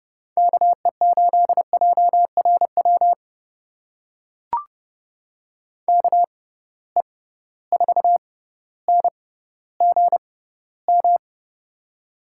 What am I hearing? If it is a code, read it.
Morse code, heard as a keyed tone, about 30 wpm, spaced as for 15 wpm, 700 Hz. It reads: KE8JRW KE4NGM
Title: USA Call Signs - 6 Characters - 6x Character Spacing - 30wpm